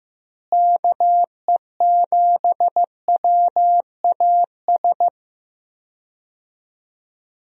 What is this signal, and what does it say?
Morse code, heard as a keyed tone, about 15 wpm, 700 Hz. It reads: KE7WAS
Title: USA Call Signs - 6 Characters - Repeated 3x - 15wpm